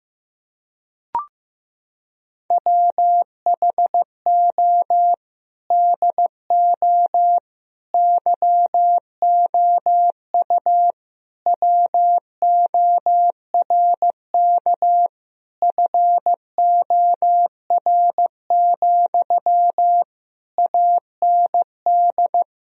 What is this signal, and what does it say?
Morse code, heard as a keyed tone, about 15 wpm, 700 Hz. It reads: WHO DO YOU WORK FOR, AND
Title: Sentences from Top 1000 Words - Review Part 4 - 15wpm